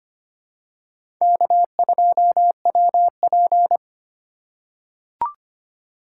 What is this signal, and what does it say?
Morse code, heard as a keyed tone, about 25 wpm, 700 Hz. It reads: K2WP
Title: USA Call Signs - 4 Characters - 2x Character Spacing - 25wpm